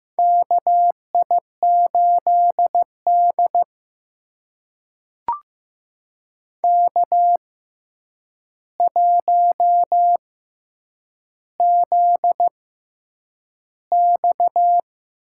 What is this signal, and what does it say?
Morse code, heard as a keyed tone, about 15 wpm, spaced as for 8 wpm, 700 Hz. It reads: KI8D K1ZX
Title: USA Call Signs - 6x Character Spacing - 15wpm